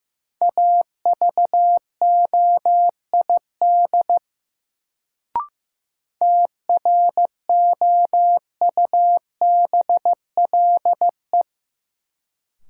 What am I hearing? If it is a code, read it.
Morse code, heard as a keyed tone, about 15 wpm, 700 Hz. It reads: AVOID TROUBLE
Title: Sentences from Top 1200 Words - New Word Drill Alt - 15wpm